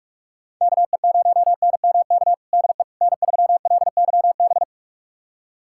Morse code, heard as a keyed tone, about 46 wpm, 700 Hz. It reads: KE0NMK DE N3LYB
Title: USA Callsign DE Callsign 45wpm